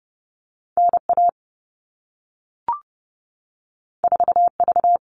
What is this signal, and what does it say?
Morse code, heard as a keyed tone, about 30 wpm, 700 Hz. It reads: NA 4V
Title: All Cut Numbers - 2-Digit 30wpm